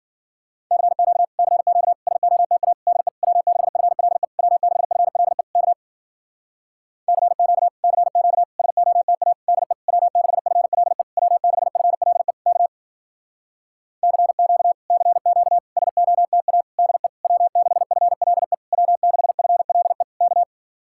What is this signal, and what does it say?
Morse code, heard as a keyed tone, about 53 wpm, 700 Hz. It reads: CQ CQ SOTA DE W6FLE W6FLE K CQ CQ SOTA DE W6FLE W6FLE K CQ CQ SOTA DE W6FLE W6FLE K